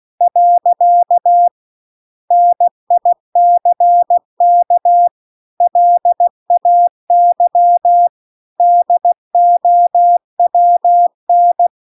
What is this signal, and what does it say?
Morse code, heard as a keyed tone, about 16 wpm, 700 Hz. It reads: . NICK LAY DOWN